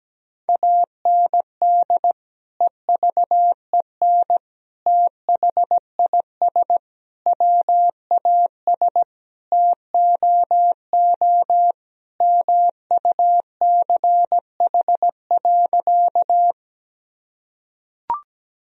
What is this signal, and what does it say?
Morse code, heard as a keyed tone, about 17 wpm, 700 Hz. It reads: AND EVEN THIS WAS TOO MUCH.
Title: Sentences from Top 1000 Words - Review Part 1 - 17wpm